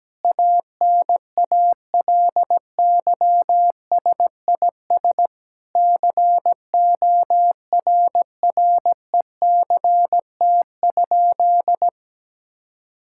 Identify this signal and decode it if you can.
Morse code, heard as a keyed tone, about 17 wpm, 700 Hz. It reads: ANALYSIS CORRECT?
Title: Sentences from Top 1000 Words - Review Part 5 - 17wpm